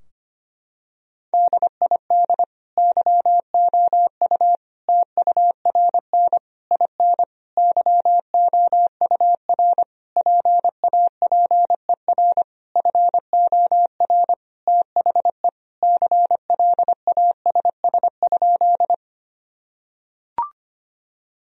Morse code, heard as a keyed tone, about 25 wpm, 700 Hz. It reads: DID YOU TURN IN YOUR PAPER FOR THE CLASS?
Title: Sentences from Top 1000 Words - Review Part 2 - 25wpm